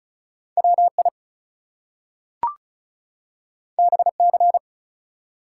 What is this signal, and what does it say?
Morse code, heard as a keyed tone, about 36 wpm, 700 Hz. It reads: WI BC
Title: US States and Canadian Provinces 35wpm